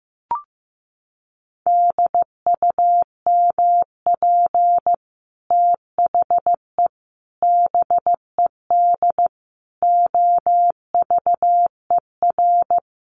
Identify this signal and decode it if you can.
Morse code, heard as a keyed tone, about 15 wpm, 700 Hz. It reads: DUMP THE BED OVER
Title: Sentences from Top 1300 Words 15wpm